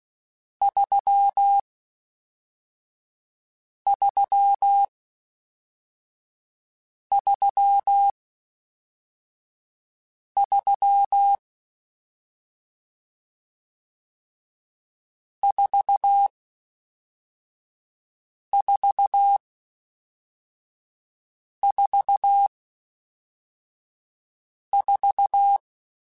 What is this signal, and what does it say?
Morse code, heard as a keyed tone, about 16 wpm, spaced as for 4 wpm, 800 Hz. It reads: 3333 4444